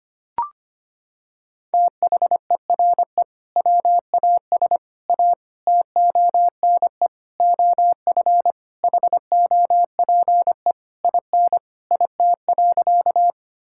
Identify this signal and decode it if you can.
Morse code, heard as a keyed tone, about 25 wpm, 700 Hz. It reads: THERE WAS A TONE OF HOPE IN IT.
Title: Sentences from Top 1600 Words 25wpm